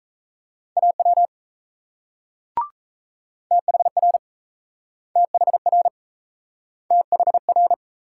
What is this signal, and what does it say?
Morse code, heard as a keyed tone, about 38 wpm, 700 Hz. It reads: AW THR THR THR